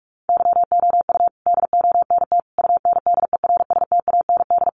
Morse code, heard as a keyed tone, about 44 wpm, 700 Hz. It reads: YOU DONT UNDERSTAND
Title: Sentences from Top 1000 Words 45wpm